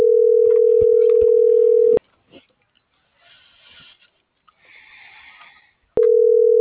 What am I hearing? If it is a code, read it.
Ringback tone (North America), 440 + 480 Hz.